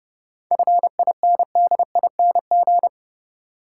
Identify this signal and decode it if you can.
Morse code, heard as a keyed tone, about 30 wpm, 700 Hz. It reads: FINDING
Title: Sentences from Top 1100 Words - New Word Drill - 30wpm